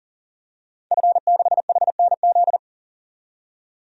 Morse code, heard as a keyed tone, about 40 wpm, 700 Hz. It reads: F6HNZ
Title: DXCC-100 Call Signs - 5 Characters - 40wpm